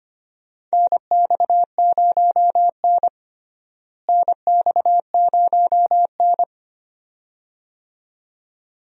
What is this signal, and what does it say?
Morse code, heard as a keyed tone, about 25 wpm, 700 Hz. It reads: NX0N NX0N
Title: USA Call Signs - 4 Characters - Repeated 2x - 25wpm